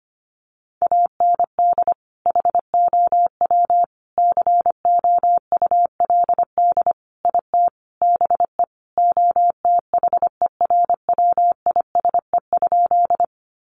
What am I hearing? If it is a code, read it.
Morse code, heard as a keyed tone, about 25 wpm, 700 Hz. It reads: AND HOW COULD IT BE OTHERWISE?
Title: Sentences from Top 1200 Words 25wpm